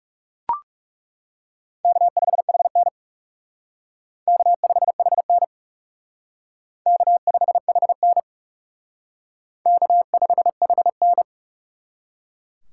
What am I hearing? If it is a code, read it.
Morse code, heard as a keyed tone, about 37 wpm, 700 Hz. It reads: K5HN K5HN K5HN K5HN
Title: Speed Racing - USA 1x2 Call Signs 30wpm